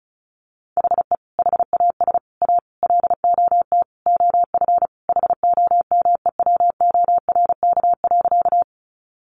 Morse code, heard as a keyed tone, about 34 wpm, 700 Hz. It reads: HE HAS A LOT OF HOMEWORK.
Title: Additional Easy Phrases 35wpm